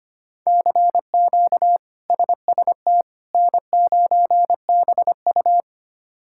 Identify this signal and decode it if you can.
Morse code, heard as a keyed tone, about 25 wpm, 700 Hz. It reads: CQ SST N9BU
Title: K1USN SST QSO Practice - 25wpm